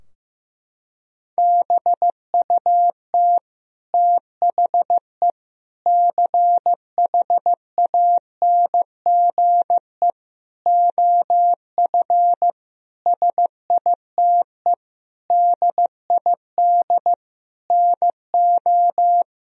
Morse code, heard as a keyed tone, about 15 wpm, 700 Hz. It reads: BUT THE CHANGE OF SITE DID NO
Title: Sentences from Top 1000 Words - Review Part 4 - 15wpm